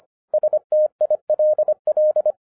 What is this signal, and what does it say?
Morse code, heard as a keyed tone, about 24 wpm, 610 Hz. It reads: STILL
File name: 2011-10-mp3 / n04-JUST MOVED 25wpm